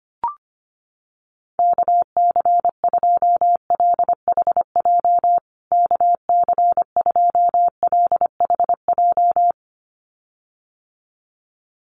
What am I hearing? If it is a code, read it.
Morse code, heard as a keyed tone, about 25 wpm, 700 Hz. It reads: KC2LHJ KC2LHJ
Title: USA Call Signs Repeated 2x 25wpm